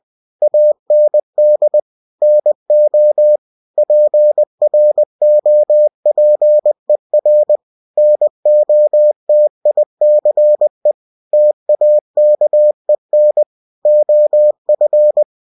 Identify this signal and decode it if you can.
Morse code, heard as a keyed tone, about 20 wpm, 600 Hz. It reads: AND NO PROPER NOTICE TAKEN OF
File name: cw-books / winnie-the-pooh / winnie-20wpm-page-0048